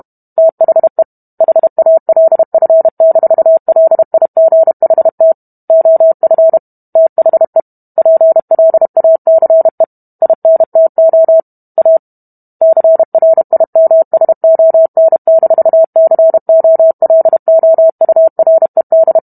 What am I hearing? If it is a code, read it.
Morse code, heard as a keyed tone, about 32 wpm, 650 Hz. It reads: THE HALF-LIGHT OF THE PLACE INTO A CRIMSON-COLOURED